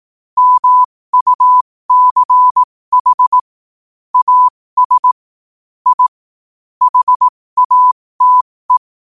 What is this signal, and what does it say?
Morse code, heard as a keyed tone, about 18 wpm, spaced as for 15 wpm, 1000 Hz. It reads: MUCH AS I HATE
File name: qotdfiles / 15wpm